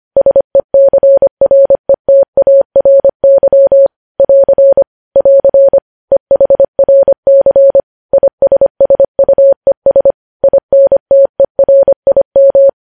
Morse code, heard as a keyed tone, about 25 wpm, 550 Hz. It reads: SECRETARY <AR> <AR> EHRC ISSUES INTERIM
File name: podcast / News headlines-1x-25x25